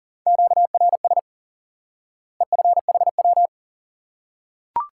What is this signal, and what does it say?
Morse code, heard as a keyed tone, about 40 wpm, 700 Hz. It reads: QRS EFHW
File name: IQE Course 3 - Lesson 89 - YAGI - Two Elements - 40wpm